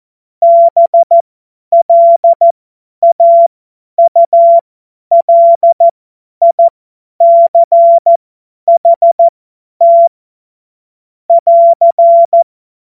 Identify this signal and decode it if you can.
Morse code, heard as a keyed tone, about 14 wpm, spaced as for 10 wpm, 680 Hz. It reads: BLAULICHT <AR>
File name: Morsekurs / Witze / Witz 01. Juni 2021, 50 F70